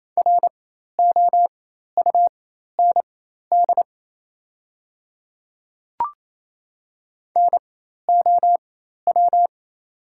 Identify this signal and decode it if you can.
Morse code, heard as a keyed tone, about 28 wpm, spaced as for 13 wpm, 700 Hz. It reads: ROUND NOW
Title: 4x Character Spacing - Top 200 Words - 28wpm